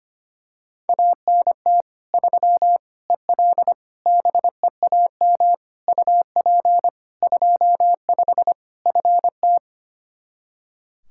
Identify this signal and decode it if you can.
Morse code, heard as a keyed tone, about 25 wpm, 700 Hz. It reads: ANT 3 EL BEAM UP 25 FT
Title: Practice QSO - Exchange 2 - Info - 25wpm